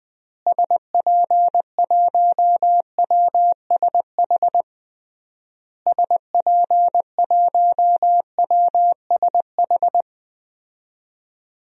Morse code, heard as a keyed tone, about 20 wpm, 700 Hz. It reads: SP1WSH SP1WSH
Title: DXCC-100 Call Signs - 6 Characters - Repeated 3x - 20wpm